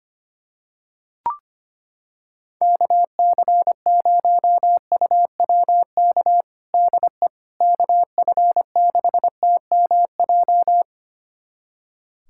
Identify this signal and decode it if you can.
Morse code, heard as a keyed tone, about 25 wpm, 700 Hz. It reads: KC0UWK DE KF6TMJ
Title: USA Callsign DE Callsign - 6 Characters - 25wpm